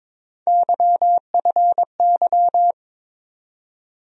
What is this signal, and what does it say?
Morse code, heard as a keyed tone, about 22 wpm, 700 Hz. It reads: YFY
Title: USA Call Sign Postfixes - 3 Letters 22wpm